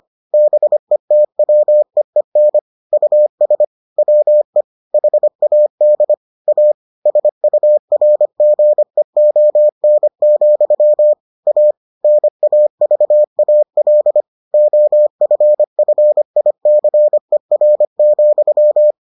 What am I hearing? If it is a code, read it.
Morse code, heard as a keyed tone, about 25 wpm, 600 Hz. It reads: BETWEEN US WE HAD A SURGEON, A NAVAL OFFICER,